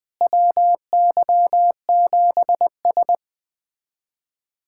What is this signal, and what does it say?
Morse code, heard as a keyed tone, about 21 wpm, 700 Hz. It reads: WY7S
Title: USA Callsign DE Callsign - 4 Characters - 20wpm